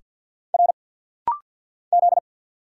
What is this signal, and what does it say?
Morse code, heard as a keyed tone, about 50 wpm, 700 Hz. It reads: R Z
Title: Single Letter-Number Alt - Rapid-Fire 50wpm